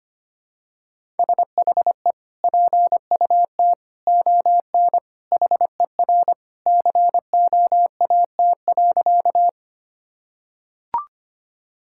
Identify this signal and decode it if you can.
Morse code, heard as a keyed tone, about 25 wpm, 700 Hz. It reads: SHE PUT ON HER COAT.
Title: Sentences from Top 1600 Words 25wpm